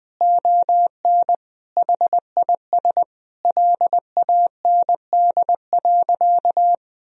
Morse code, heard as a keyed tone, about 20 wpm, 700 Hz. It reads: ON HIS LAND.
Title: Sentences from Top 1000 Words - Review Part 4 - 20wpm